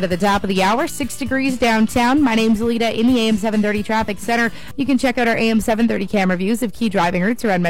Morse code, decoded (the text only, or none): none